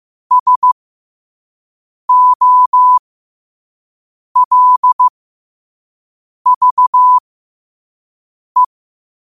Morse code, heard as a keyed tone, about 15 wpm, spaced as for 5 wpm, 1000 Hz. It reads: SOLVE
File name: qotdfiles / 5wpm